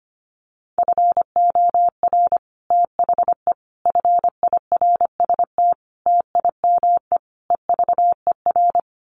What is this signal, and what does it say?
Morse code, heard as a keyed tone, about 25 wpm, 700 Hz. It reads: FOR THE FIRST TIME EVER
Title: Sets of 5 Words Encore2 25wpm